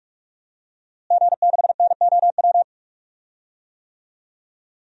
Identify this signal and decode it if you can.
Morse code, heard as a keyed tone, about 46 wpm, 700 Hz. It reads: G6NOW
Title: DXCC-100 Call Signs - 5 Characters - Repeated 3x - 45wpm